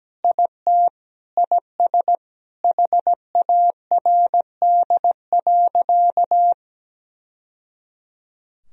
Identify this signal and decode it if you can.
Morse code, heard as a keyed tone, about 17 wpm, 700 Hz. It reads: IT IS HARD.
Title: Sentences from Top 1000 Words - Review Part 3 - 17wpm